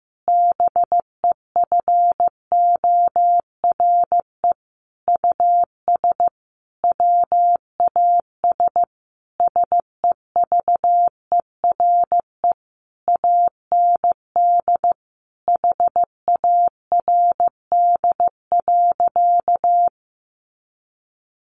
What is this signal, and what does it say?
Morse code, heard as a keyed tone, about 15 wpm, 700 Hz. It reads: BEFORE US WAS SEVERE AND HARD.
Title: Sentences from Top 1400 Words 15wpm